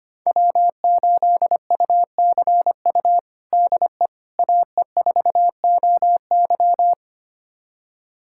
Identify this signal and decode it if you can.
Morse code, heard as a keyed tone, about 25 wpm, 700 Hz. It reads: W8UCU DE AE4OY
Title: USA Callsign DE Callsign 25wpm